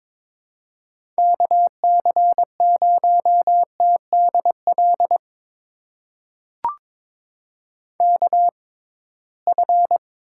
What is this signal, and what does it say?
Morse code, heard as a keyed tone, about 21 wpm, spaced as for 16 wpm, 700 Hz. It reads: KC0TDL KF
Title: USA Call Signs - 6 Characters - 6x Character Spacing - 22wpm